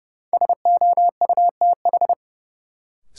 Morse code, heard as a keyed tone, about 30 wpm, 700 Hz. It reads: SOUTH
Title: Sentences from Top 400 Words - New Word Drill - 30wpm